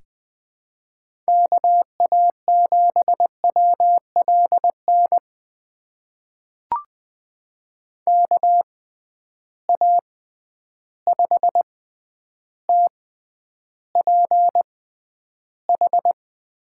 Morse code, heard as a keyed tone, about 20 wpm, spaced as for 10 wpm, 700 Hz. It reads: KA7WLN KA5TPH
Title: USA Call Signs - 6 Characters - 6x Character Spacing - 20wpm